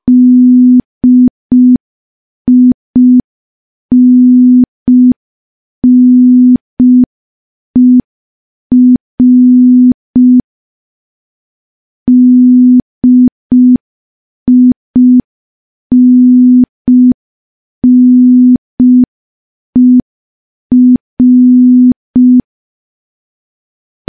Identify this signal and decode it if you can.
Morse code, heard as a keyed tone, about 5 wpm, 250 Hz. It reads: DINNER DINNER
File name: morse / morse-training-05WPM